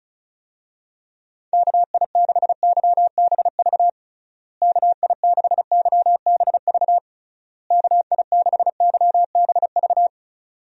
Morse code, heard as a keyed tone, about 34 wpm, 700 Hz. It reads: KI6YBV KI6YBV KI6YBV
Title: USA Call Signs - 6 Characters - Repeated 3x - 35wpm